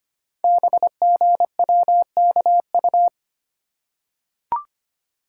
Morse code, heard as a keyed tone, about 25 wpm, 700 Hz. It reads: BGWKU
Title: Code Groups - 5 Letters - 25wpm